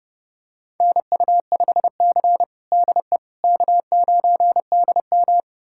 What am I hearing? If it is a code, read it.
Morse code, heard as a keyed tone, about 30 wpm, 700 Hz. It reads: NU5C DE K9DM
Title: USA Callsign DE Callsign - 4 Characters - 30wpm